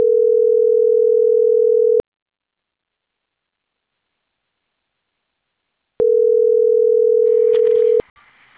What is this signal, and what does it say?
Ringback tone (North America), 440 + 480 Hz.